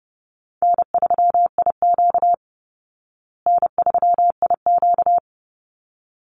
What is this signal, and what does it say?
Morse code, heard as a keyed tone, about 30 wpm, 700 Hz. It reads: N3IQ N3IQ
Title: Easy Call Signs Repeated 2x - 30wpm